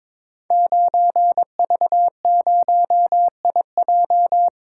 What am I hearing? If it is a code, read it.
Morse code, heard as a keyed tone, about 22 wpm, 700 Hz. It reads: 9V0IJ
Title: Code Groups - 5 Letter:Numbers - 22wpm